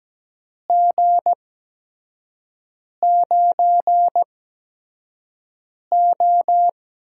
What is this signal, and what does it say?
Morse code, heard as a keyed tone, about 17 wpm, spaced as for 5 wpm, 700 Hz. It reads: G9O
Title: USA Call Signs - 4 Characters - 8x Character Spacing - 17wpm